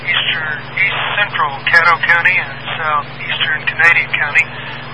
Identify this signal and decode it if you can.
Emergency Alert System (SAME) tones, heard as harsh buzzing bursts, with speech. It is heard from 0.4 to 4.1 s.